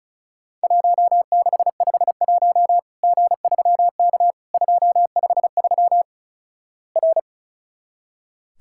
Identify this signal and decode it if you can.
Morse code, heard as a keyed tone, about 36 wpm, 700 Hz. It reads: 1651 G3K 253 R
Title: Worked All Europe DX Contest - QTC Exchange - 35wpm